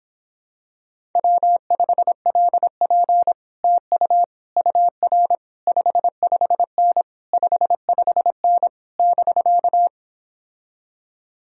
Morse code, heard as a keyed tone, about 26 wpm, 700 Hz. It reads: W5LP TU UR 55N 55N <BK>